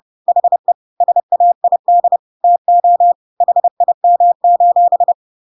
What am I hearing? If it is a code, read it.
Morse code, heard as a keyed tone, about 30 wpm, 700 Hz. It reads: HE SAID TO HIM:
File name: Pinocchio / Pinocchio ch5 30 30 ik1hge